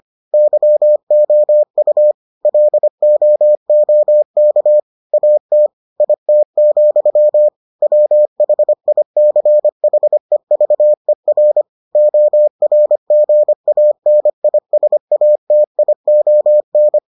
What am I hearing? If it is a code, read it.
Morse code, heard as a keyed tone, about 25 wpm, 600 Hz. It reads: YOU LOOK AT IT, WHICHEVER ORGANISATION